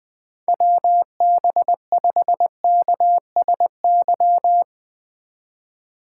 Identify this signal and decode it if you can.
Morse code, heard as a keyed tone, about 20 wpm, 700 Hz. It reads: WB5KSY
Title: USA Call Signs - 6 Characters - 3x Character Spacing - 20wpm